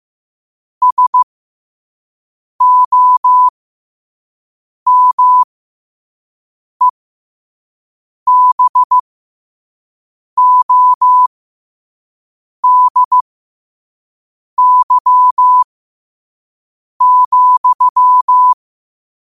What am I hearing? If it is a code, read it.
Morse code, heard as a keyed tone, about 15 wpm, spaced as for 5 wpm, 1000 Hz. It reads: SOMEBODY,
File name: qotdfiles / 5wpm